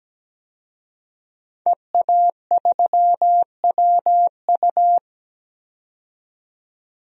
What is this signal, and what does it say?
Morse code, heard as a keyed tone, about 17 wpm, 700 Hz. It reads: EA3WU
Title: DXCC-100 Call Signs - 5 Characters - 17wpm